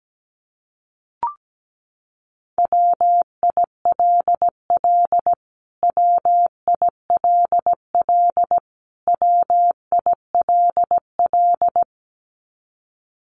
Morse code, heard as a keyed tone, about 17 wpm, 700 Hz. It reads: WILL WILL WILL
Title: Top 500 Words Repeated 17wpm